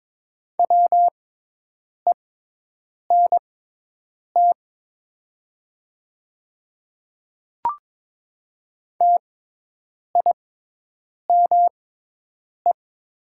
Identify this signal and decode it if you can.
Morse code, heard as a keyed tone, about 22 wpm, spaced as for 8 wpm, 700 Hz. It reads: WENT TIME
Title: 6x Character Spacing - Top 300 Words - 22wpm